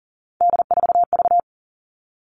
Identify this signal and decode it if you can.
Morse code, heard as a keyed tone, about 40 wpm, 700 Hz. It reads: D4V
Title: All Cut Numbers - 3-Digit 40wpm